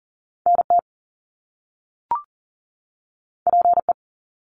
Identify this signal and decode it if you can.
Morse code, heard as a keyed tone, about 41 wpm, 700 Hz. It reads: NT PE
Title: Canadian Provinces and Territories 40wpm